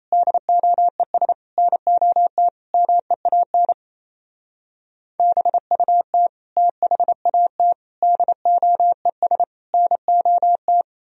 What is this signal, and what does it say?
Morse code, heard as a keyed tone, about 35 wpm, 700 Hz. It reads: DOES NOT MEAN BUT THAT DOES NOT
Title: Speed-Racing - Sets of 5 Words 28wpm